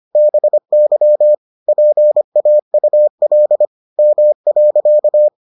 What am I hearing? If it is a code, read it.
Morse code, heard as a keyed tone, about 25 wpm, 600 Hz. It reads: BY PAUL M.